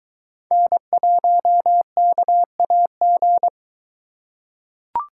Morse code, heard as a keyed tone, about 22 wpm, 700 Hz. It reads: N1KAG